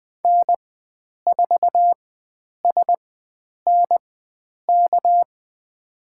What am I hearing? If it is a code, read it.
Morse code, heard as a keyed tone, about 20 wpm, spaced as for 9 wpm, 700 Hz. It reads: N4SNK
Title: USA Call Signs - 5 Characters - 4x Character Spacing - 20wpm